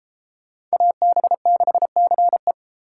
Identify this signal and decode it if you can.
Morse code, heard as a keyed tone, about 32 wpm, 700 Hz. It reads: AB6CE